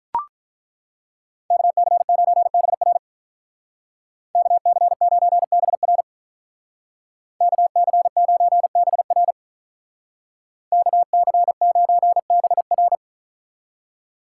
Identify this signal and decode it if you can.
Morse code, heard as a keyed tone, about 43 wpm, 700 Hz. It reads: KC9BR KC9BR KC9BR KC9BR
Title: Speed Racing - USA 2x2 Call Signs 35wpm